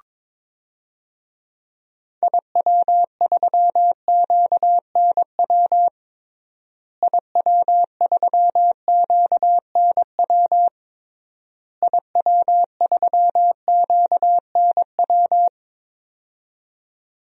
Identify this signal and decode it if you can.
Morse code, heard as a keyed tone, about 22 wpm, 700 Hz. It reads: IW3QNW IW3QNW IW3QNW
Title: DXCC-100 Call Signs - 6 Characters - Repeated 3x - 22wpm